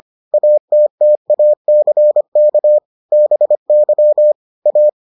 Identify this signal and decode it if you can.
Morse code, heard as a keyed tone, about 25 wpm, 600 Hz. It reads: ATTACK BY A